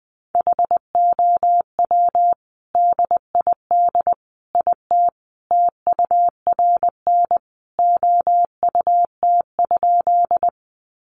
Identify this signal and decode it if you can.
Morse code, heard as a keyed tone, about 20 wpm, 700 Hz. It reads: HOW DID IT TURN OUT?
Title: A Sound Of Thunder 20wpm